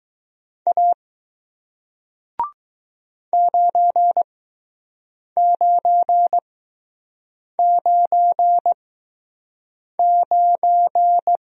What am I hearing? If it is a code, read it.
Morse code, heard as a keyed tone, about 20 wpm, spaced as for 6 wpm, 700 Hz. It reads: A 9999